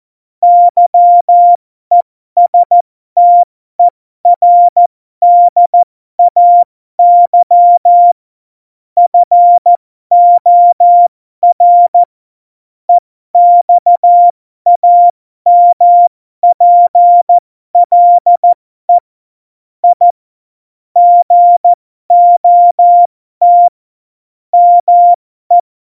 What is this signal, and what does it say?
Morse code, heard as a keyed tone, about 14 wpm, spaced as for 12 wpm, 700 Hz. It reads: YESTERDAY FOR EXAMPLE I GOT ME
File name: Morsekurs / Witze